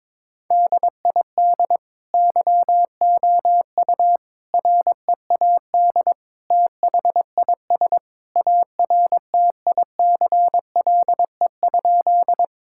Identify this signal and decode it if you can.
Morse code, heard as a keyed tone, about 22 wpm, 700 Hz. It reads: DID YOU READ THIS ARTICLE?
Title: Sentences from Top 1000 Words - Review Part 5 - 22wpm